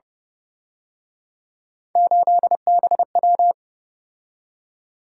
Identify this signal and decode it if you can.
Morse code, heard as a keyed tone, about 29 wpm, 700 Hz. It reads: 8BW